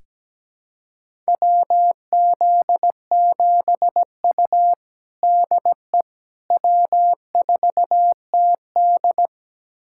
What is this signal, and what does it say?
Morse code, heard as a keyed tone, about 17 wpm, 700 Hz. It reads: WZ7U DE W4TD
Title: USA Callsign DE Callsign - 4 Characters - 17wpm